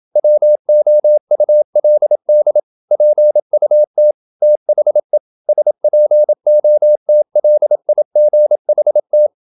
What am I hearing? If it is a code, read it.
Morse code, heard as a keyed tone, about 27 wpm, 600 Hz. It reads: WOULD PUT THE SPOTLIGHT
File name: Never 0027wpm